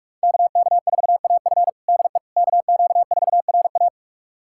Morse code, heard as a keyed tone, about 44 wpm, 700 Hz. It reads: KK4AF DE KQ4RA
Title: USA Callsign DE Callsign - 5 Characters - 45wpm